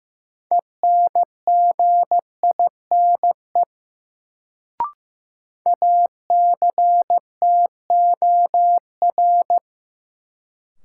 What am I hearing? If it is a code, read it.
Morse code, heard as a keyed tone, about 15 wpm, 700 Hz. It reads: ENGINE ACTOR
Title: Sentences from Top 1500 Words - New Word Drill - 15wpm